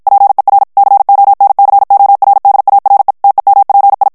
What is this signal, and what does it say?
Morse code, heard as a keyed tone, about 54 wpm, 800 Hz. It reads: PER CONCORDARE TEMPI
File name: ITALIEN texte clair 1 vitesse 50 mots minute